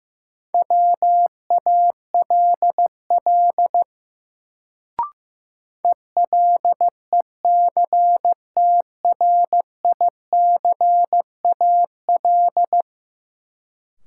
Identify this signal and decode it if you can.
Morse code, heard as a keyed tone, about 15 wpm, 700 Hz. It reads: WALL ELECTRICAL
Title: Sentences from Top 1000 Words - Review Part 6 - New Word Drill - 15wpm